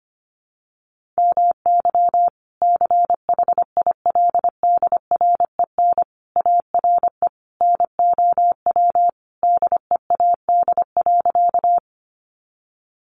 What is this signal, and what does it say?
Morse code, heard as a keyed tone, about 25 wpm, 700 Hz. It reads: MY CHILDREN ARE NOW DEAD.
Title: Sentences from Top 1500 Words 25wpm